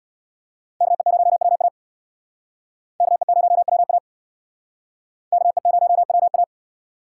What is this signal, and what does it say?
Morse code, heard as a keyed tone, about 65 wpm, 700 Hz. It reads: KE0WA KE0WA KE0WA